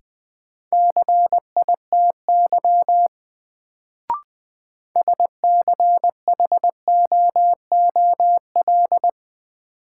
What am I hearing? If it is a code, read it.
Morse code, heard as a keyed tone, about 20 wpm, 700 Hz. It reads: CITY SCHOOL
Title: Sets of 2 Words - New Word Drill - 20wpm